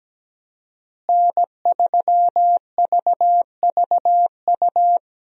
Morse code, heard as a keyed tone, about 17 wpm, 700 Hz. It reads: N3VVU
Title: USA Call Signs - 5 Characters - Repeated 3x - 17wpm